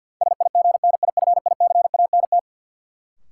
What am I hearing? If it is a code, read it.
Morse code, heard as a keyed tone, about 53 wpm, 700 Hz. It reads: SIGNIFICANT
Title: Sentences from Top 1000 Words - Review Part 6 - New Word Drill - 50wpm